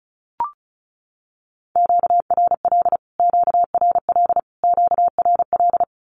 Morse code, heard as a keyed tone, about 34 wpm, 700 Hz. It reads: QRL QRL QRL
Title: Q-Codes Repeated 3x 35wpm